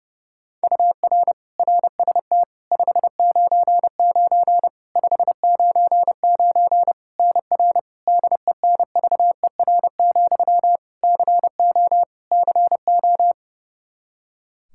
Morse code, heard as a keyed tone, about 32 wpm, 700 Hz. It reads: UR RST 599 599 NR DENVER, CO CO
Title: Practice QSO - Exchange 1 - Beginning - 30wpm